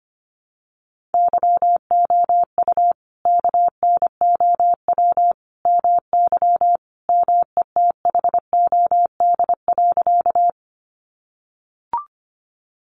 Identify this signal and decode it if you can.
Morse code, heard as a keyed tone, about 25 wpm, 700 Hz. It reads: YOU KNOW MY METHOD.
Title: Sentences from Top 600 Words 25wpm